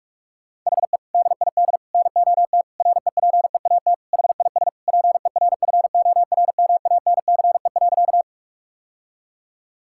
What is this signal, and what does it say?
Morse code, heard as a keyed tone, about 41 wpm, 700 Hz. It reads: HE DID NOT REPEAT HIS PERFORMANCE.
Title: Sentences from Top 1600 Words 45wpm